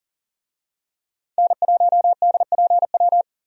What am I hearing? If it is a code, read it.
Morse code, heard as a keyed tone, about 38 wpm, 700 Hz. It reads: N1DPW
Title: USA Call Signs - 5 Characters - 40wpm